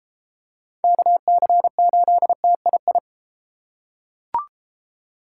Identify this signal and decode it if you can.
Morse code, heard as a keyed tone, about 32 wpm, 700 Hz. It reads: KC8TII